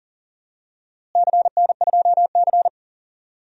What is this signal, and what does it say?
Morse code, heard as a keyed tone, about 37 wpm, 700 Hz. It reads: CN2C
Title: DXCC-100 Call Signs - 4 Characters - Repeated 3x - 40wpm